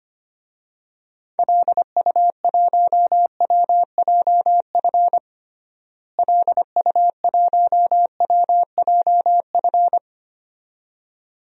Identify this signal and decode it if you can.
Morse code, heard as a keyed tone, about 25 wpm, 700 Hz. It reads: LU1WJF LU1WJF
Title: DXCC-100 Call Signs - 6 Characters - Repeated 2x - 25wpm